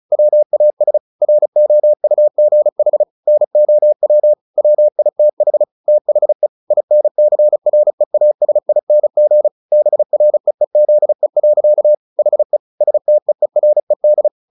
Morse code, heard as a keyed tone, about 34 wpm, 600 Hz. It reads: WAS ROUGH NOW WITH THE INCREASING BREEZE. HE STEERED
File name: the old man and the sea 2 35wpm